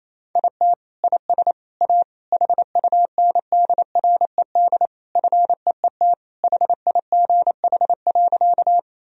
Morse code, heard as a keyed tone, about 28 wpm, 700 Hz. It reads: IT IS A HUNDRED FEET HIGH.
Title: Sentences from Top 1000 Words - Review Part 2 - 28wpm